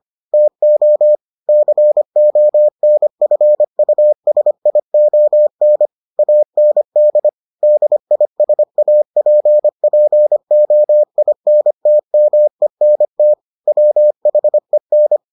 Morse code, heard as a keyed tone, about 25 wpm, 600 Hz. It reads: TO CONFUSION AND DISAPPOINTMENT WHEN